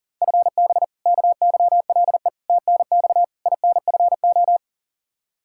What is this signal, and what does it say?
Morse code, heard as a keyed tone, about 40 wpm, 700 Hz. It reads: FB KYLE TNX INFO
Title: Practice QSO - Exchange 3 - More Info - 40wpm